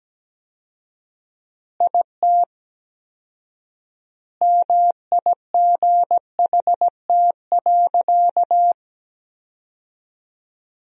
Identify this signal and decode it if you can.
Morse code, heard as a keyed tone, about 17 wpm, 700 Hz. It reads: IT MIGHT.
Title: 4x Word-Spacing - Sentences from Top 500 Words - 17wpm